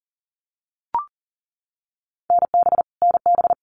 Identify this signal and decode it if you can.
Morse code, heard as a keyed tone, about 40 wpm, 700 Hz. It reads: NB NB
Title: Canadian Provinces and Territories - Repeated 3x - 40wpm